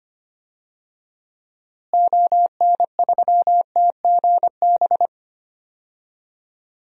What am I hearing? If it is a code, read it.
Morse code, heard as a keyed tone, about 25 wpm, 700 Hz. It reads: ON3TGB
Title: DXCC-100 Call Signs - 6 Characters - Repeated 2x - 25wpm